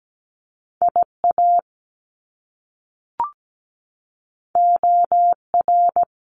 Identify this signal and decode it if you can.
Morse code, heard as a keyed tone, about 17 wpm, 700 Hz. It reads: IA OR
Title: US States 17wpm